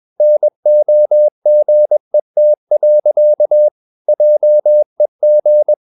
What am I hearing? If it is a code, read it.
Morse code, heard as a keyed tone, about 21 wpm, 600 Hz. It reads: NOGET. JEG
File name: Ferie QRS 21wpm